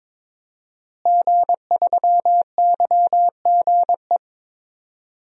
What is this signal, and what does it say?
Morse code, heard as a keyed tone, about 22 wpm, 700 Hz. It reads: G3YGE
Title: DXCC-100 Call Signs - 5 Characters - Repeated 2x - 22wpm